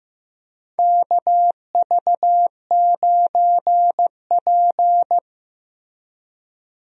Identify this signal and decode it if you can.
Morse code, heard as a keyed tone, about 15 wpm, 700 Hz. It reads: KV9P
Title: USA Call Signs - 4 Characters - Repeated 3x - 15wpm